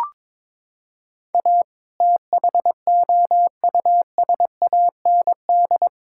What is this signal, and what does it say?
Morse code, heard as a keyed tone, about 22 wpm, 700 Hz. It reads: A THOUSAND
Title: Sentences from Top 1000 Words - Review Part 6 - 22wpm